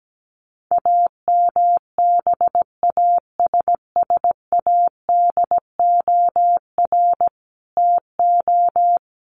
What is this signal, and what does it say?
Morse code, heard as a keyed tone, about 17 wpm, 700 Hz. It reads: AMBASSADOR TO
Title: Sets of 5 Words Encore2 17wpm